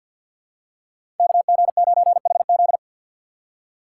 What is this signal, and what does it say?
Morse code, heard as a keyed tone, about 51 wpm, 700 Hz. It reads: KG9HZ
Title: USA Call Signs - 5 Characters - 4x Character Spacing - 50wpm